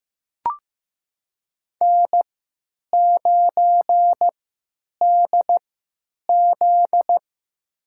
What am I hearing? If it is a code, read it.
Morse code, heard as a keyed tone, about 15 wpm, spaced as for 9 wpm, 700 Hz. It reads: N9DZ
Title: USA Call Signs - 4 Characters - 3x Character Spacing - 15wpm